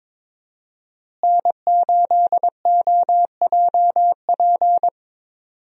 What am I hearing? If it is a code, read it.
Morse code, heard as a keyed tone, about 22 wpm, 700 Hz. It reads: N8OJP
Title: USA Call Signs - 5 Characters - Repeated 3x - 22wpm